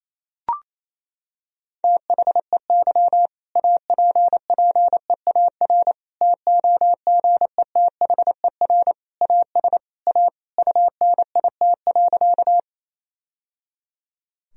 Morse code, heard as a keyed tone, about 28 wpm, 700 Hz. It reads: THEY APPEAR TOGETHER AS A UNIT.
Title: Sentences from Top 1000 Words - Review Part 2 - 28wpm